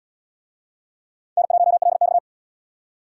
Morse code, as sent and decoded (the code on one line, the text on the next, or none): -. .---- .-.. -..-
N1LX